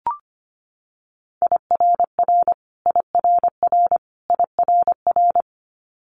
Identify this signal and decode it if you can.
Morse code, heard as a keyed tone, about 25 wpm, 700 Hz. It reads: IRR IRR IRR
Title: Multi-letter Phonemes Repeated 3x 25wpm